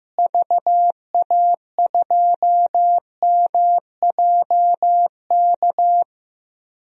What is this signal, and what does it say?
Morse code, heard as a keyed tone, about 16 wpm, 700 Hz. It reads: VA2MJK
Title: DXCC-100 Call Signs - 6 Characters - Repeated 3x - 15wpm